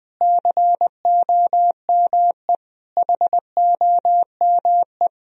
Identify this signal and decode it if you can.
Morse code, heard as a keyed tone, about 19 wpm, 700 Hz. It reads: COME HOME
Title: Sentences from Top 1000 Words - Review Part 4 - 20wpm